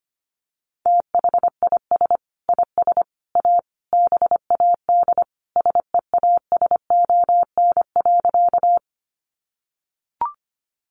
Morse code, heard as a keyed tone, about 25 wpm, 700 Hz. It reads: THIS IS A BAD SEASON.
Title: Sentences from Top 1400 Words 25wpm